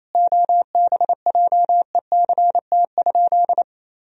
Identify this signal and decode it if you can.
Morse code, heard as a keyed tone, about 28 wpm, 700 Hz. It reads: OBJECT?
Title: Sentences from Top 500 Words and Abbreviations 28wpm